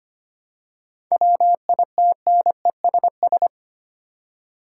Morse code, heard as a keyed tone, about 25 wpm, 700 Hz. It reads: WITNESS
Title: Seven Letter Words - 25wpm